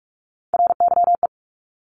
Morse code, heard as a keyed tone, about 46 wpm, 700 Hz. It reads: RYE
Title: Multi-letter Phonemes 45wpm